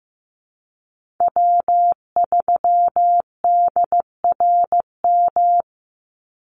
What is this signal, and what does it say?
Morse code, heard as a keyed tone, about 15 wpm, 700 Hz. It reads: W3DRM
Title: USA Call Signs 15wpm